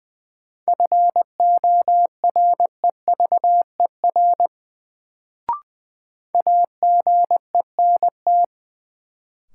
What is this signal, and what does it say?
Morse code, heard as a keyed tone, about 21 wpm, 700 Hz. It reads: FOREVER AGENT
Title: Sentences from Top 1600 Words - New Word Drill Alt - 20wpm